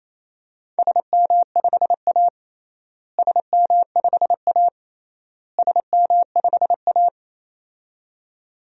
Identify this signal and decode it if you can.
Morse code, heard as a keyed tone, about 28 wpm, 700 Hz. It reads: SM5A SM5A SM5A
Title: DXCC-100 Call Signs - 4 Characters - Repeated 3x - 28wpm